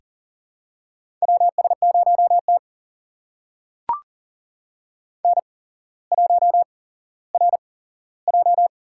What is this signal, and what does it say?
Morse code, heard as a keyed tone, about 40 wpm, spaced as for 17 wpm, 700 Hz. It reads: WS0T N1RJ
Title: USA Call Signs - 4 Characters - 8x Character Spacing - 40wpm